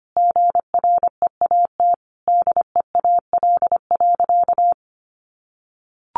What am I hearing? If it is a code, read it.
Morse code, heard as a keyed tone, about 25 wpm, 700 Hz. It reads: GREAT DEAL.
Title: Sentences from Top 1100 Words 25wpm